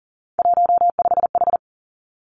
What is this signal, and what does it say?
Morse code, heard as a keyed tone, about 42 wpm, 700 Hz. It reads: 15H
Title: Three Challenging Letter-Number Combinations 40wpm